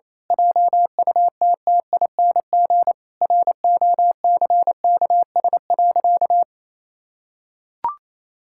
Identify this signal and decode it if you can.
Morse code, heard as a keyed tone, about 28 wpm, 700 Hz. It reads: JUTTING ROCKS.
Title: Pangram Challenge v2 - 28wpm